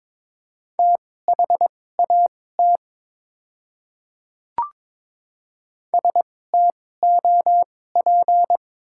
Morse code, heard as a keyed tone, about 22 wpm, spaced as for 16 wpm, 700 Hz. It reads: THAT STOP
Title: 2x Character Spacing - Top 300 Words - 22wpm